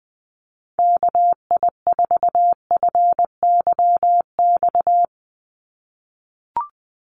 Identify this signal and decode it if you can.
Morse code, heard as a keyed tone, about 20 wpm, 700 Hz. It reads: KI4FYX
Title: USA Call Signs 20wpm